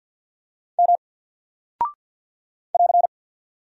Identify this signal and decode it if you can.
Morse code, heard as a keyed tone, about 50 wpm, 700 Hz. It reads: M <AR>
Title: Single Character (Letter:Number:Punctuation:Prosign) 50wpm